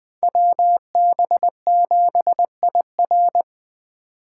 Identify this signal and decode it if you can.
Morse code, heard as a keyed tone, about 21 wpm, 700 Hz. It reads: WB7IR
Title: Speed Racing - USA 2x2 Call Signs 17wpm